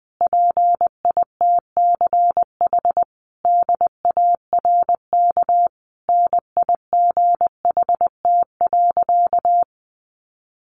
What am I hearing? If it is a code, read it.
Morse code, heard as a keyed tone, about 20 wpm, 700 Hz. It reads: PITCH DARK NIGHT.
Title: Sentences from Top 1300 Words 20wpm